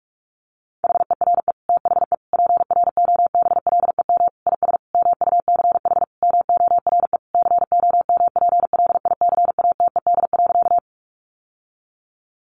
Morse code, heard as a keyed tone, about 44 wpm, 700 Hz. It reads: HERE THE PROBLEM IS MUCH MORE COMPLICATED.
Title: Sentences from Top 1200 Words 45wpm